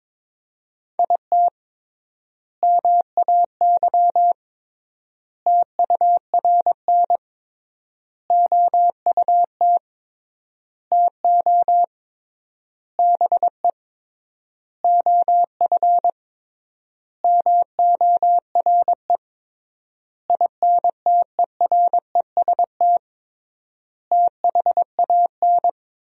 Morse code, heard as a keyed tone, about 22 wpm, 700 Hz. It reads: IT MAY TURN OUT TO BE OF MORE INTEREST THAN
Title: 3x Word-Spacing - Sentences from Top 400 Words - 22wpm